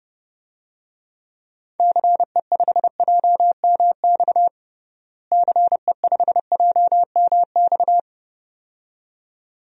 Morse code, heard as a keyed tone, about 32 wpm, 700 Hz. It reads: CE5JMX CE5JMX
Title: DXCC-100 Call Signs - 6 Characters - Repeated 2x - 30wpm